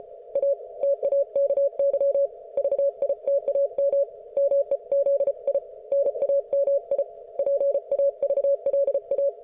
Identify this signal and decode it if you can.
Morse code, heard as a keyed tone, about 34 wpm, 560 Hz. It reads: A TAKY VITAM MEZI NAMI PAVLA